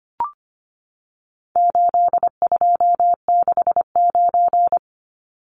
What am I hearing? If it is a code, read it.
Morse code, heard as a keyed tone, about 25 wpm, 700 Hz. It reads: 8269
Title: Numbers - 4-Digit 25wpm